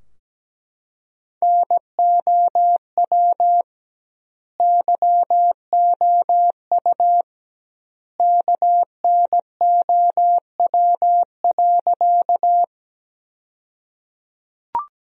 Morse code, heard as a keyed tone, about 17 wpm, 700 Hz. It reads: NOW YOU KNOW.
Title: 2x Word-Spacing - Sentences from Top 100 Words - 17wpm